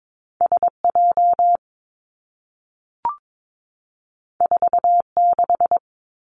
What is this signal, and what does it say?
Morse code, heard as a keyed tone, about 22 wpm, 700 Hz. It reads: S J 4 6
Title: Two Challenging Letter-Number Combinations 22wpm